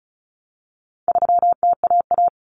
Morse code, heard as a keyed tone, about 34 wpm, 700 Hz. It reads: 3TAA
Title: Common Cut Numbers - 4-Digit 35wpm